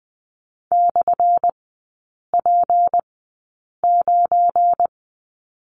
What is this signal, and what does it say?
Morse code, heard as a keyed tone, about 20 wpm, spaced as for 8 wpm, 700 Hz. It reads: /P9
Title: 178 Lesson 29e - Slash - Three Characters 20wpm